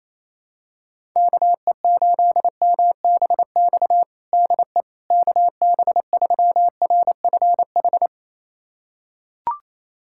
Morse code, heard as a keyed tone, about 27 wpm, 700 Hz. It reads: KE8MBX DE KB3RFH
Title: USA Callsign DE Callsign 28wpm